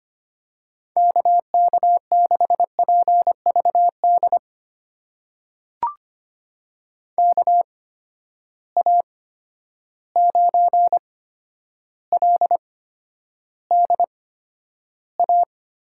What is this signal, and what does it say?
Morse code, heard as a keyed tone, about 25 wpm, spaced as for 11 wpm, 700 Hz. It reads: KK6PVD KA9LDA
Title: USA Call Signs - 6 Characters - 8x Character Spacing - 25wpm